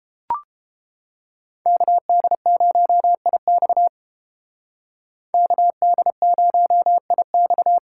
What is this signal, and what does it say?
Morse code, heard as a keyed tone, about 28 wpm, 700 Hz. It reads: KD0IX KD0IX